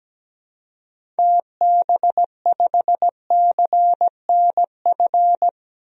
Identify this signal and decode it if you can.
Morse code, heard as a keyed tone, about 17 wpm, 700 Hz. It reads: TB5CNF
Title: DXCC-100 Call Signs - 6 Characters - Repeated 2x - 17wpm